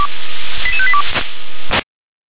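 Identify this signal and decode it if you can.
Five-tone selective call (ZVEI), heard as quick runs of five tones, tones of about 70 ms. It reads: 88582
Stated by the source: ZVEI five-tone selective call received on SDR.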